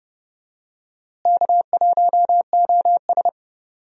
Morse code, heard as a keyed tone, about 30 wpm, 700 Hz. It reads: K1OS
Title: USA Call Signs - 4 Characters - Repeated 3x - 30wpm